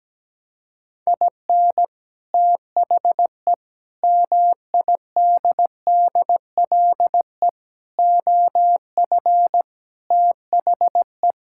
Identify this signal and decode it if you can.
Morse code, heard as a keyed tone, about 17 wpm, 700 Hz. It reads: IN THE MIDDLE OF THE
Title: Sentences from Top 1000 Words - Review Part 6 - 17wpm